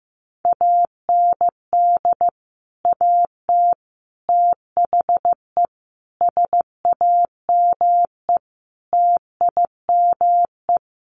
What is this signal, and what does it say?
Morse code, heard as a keyed tone, about 14 wpm, 700 Hz. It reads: AND AT THE SAME TIME
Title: Sets of 5 Words 15wpm